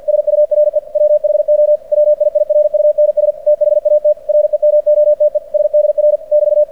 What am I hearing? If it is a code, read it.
Morse code, heard as a keyed tone, about 48 wpm, 600 Hz. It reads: SURE WHO WIIRUNS THAT REMOTE SDR <BT>
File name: archived / CRSnet